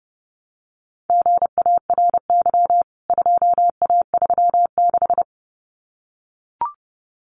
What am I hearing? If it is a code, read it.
Morse code, heard as a keyed tone, about 29 wpm, 700 Hz. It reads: GARY 2A36
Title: CWT Exchange 30wpm